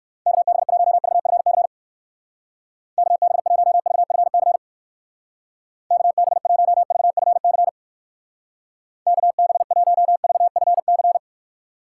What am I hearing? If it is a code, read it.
Morse code, heard as a keyed tone, about 62 wpm, 700 Hz. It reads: KB1VFC KB1VFC KB1VFC KB1VFC